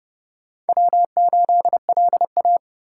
Morse code, heard as a keyed tone, about 37 wpm, 700 Hz. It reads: W8LA
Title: Speed Racing - USA 1x2 Call Signs 30wpm